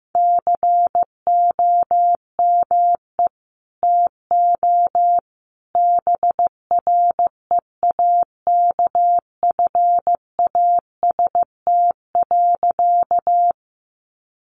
Morse code, heard as a keyed tone, about 15 wpm, 700 Hz. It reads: COME TO BREAKFAST.
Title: Sentences from Top 1500 Words 15wpm